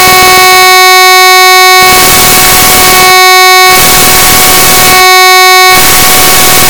Dial-up modem handshake.